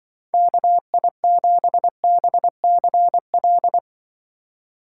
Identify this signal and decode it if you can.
Morse code, heard as a keyed tone, about 24 wpm, 700 Hz. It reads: KI7BCL
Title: Speed Racing - USA 2x3 Call Signs 20wpm